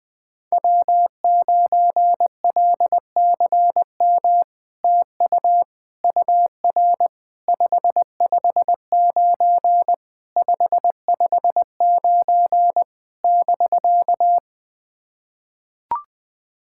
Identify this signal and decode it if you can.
Morse code, heard as a keyed tone, about 20 wpm, 690 Hz. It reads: W9LCM TU UR 559 559 <BK>
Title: SOTA QSO Practice 20wpm